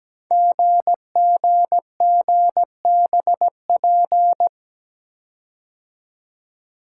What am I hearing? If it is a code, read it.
Morse code, heard as a keyed tone, about 17 wpm, 700 Hz. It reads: GGGBP
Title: Code Groups - 5 Letters - 17wpm